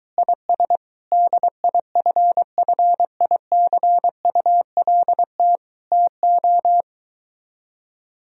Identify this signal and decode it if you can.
Morse code, heard as a keyed tone, about 23 wpm, 700 Hz. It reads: IS DIFFICULT TO